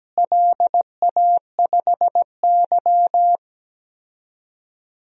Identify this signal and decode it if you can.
Morse code, heard as a keyed tone, about 17 wpm, 700 Hz. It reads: LA5Y
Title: DXCC-100 Call Signs - 4 Characters - Repeated 3x - 17wpm